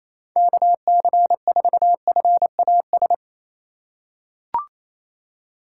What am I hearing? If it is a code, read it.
Morse code, heard as a keyed tone, about 28 wpm, 700 Hz. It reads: KC4FAS
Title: USA Call Signs - 6 Characters - 2x Character Spacing - 28wpm